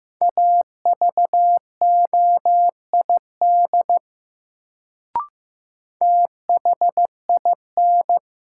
Morse code, heard as a keyed tone, about 15 wpm, 700 Hz. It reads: AVOID THIN
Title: Sentences from Top 1200 Words - New Word Drill Alt - 15wpm